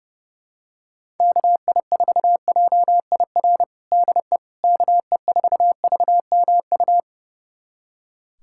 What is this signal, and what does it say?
Morse code, heard as a keyed tone, about 29 wpm, 700 Hz. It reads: KI4JIR DE KE4VMU
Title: USA Callsign DE Callsign - 6 Characters - 30wpm